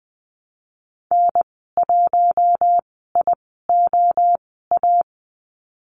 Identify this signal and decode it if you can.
Morse code, heard as a keyed tone, about 20 wpm, spaced as for 14 wpm, 700 Hz. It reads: N1IOA
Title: 042 Lesson 8g - Number 1 - Call Signs 20wpm